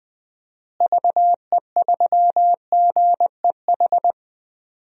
Morse code, heard as a keyed tone, about 20 wpm, 700 Hz. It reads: VE3GEH
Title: DXCC-100 Call Signs - 6 Characters - Repeated 3x - 20wpm